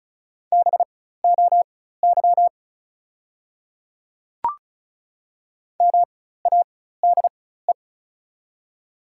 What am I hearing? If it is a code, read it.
Morse code, heard as a keyed tone, about 36 wpm, spaced as for 16 wpm, 700 Hz. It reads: BOY MADE
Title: 4x Character Spacing - Top 200 Words - 35wpm